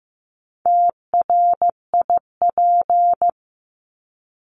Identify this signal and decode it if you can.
Morse code, heard as a keyed tone, about 15 wpm, 700 Hz. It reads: TRIP
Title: Top 1251-1500 Words 15wpm